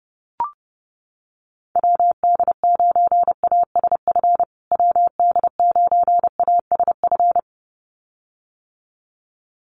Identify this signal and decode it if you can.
Morse code, heard as a keyed tone, about 28 wpm, 700 Hz. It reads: WD9ASF WD9ASF
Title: USA Call Signs Repeated 2x 30wpm